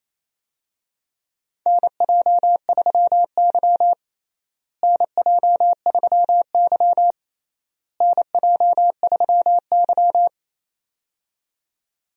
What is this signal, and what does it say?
Morse code, heard as a keyed tone, about 28 wpm, 700 Hz. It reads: NJ3Y NJ3Y NJ3Y
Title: USA Call Signs - 4 Characters - Repeated 3x - 28wpm